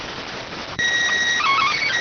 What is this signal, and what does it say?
SSTV (slow-scan television), PD 290.